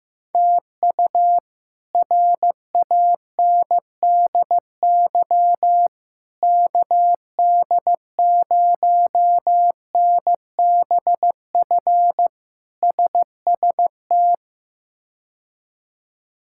Morse code, heard as a keyed tone, about 16 wpm, 700 Hz. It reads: TU RANDY KD0NBF SST
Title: K1USN SST QSO Practice - 15wpm